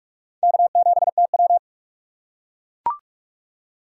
Morse code, heard as a keyed tone, about 42 wpm, 700 Hz. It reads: K7TW
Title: USA Call Signs - 4 Characters - 6x Character Spacing - 45wpm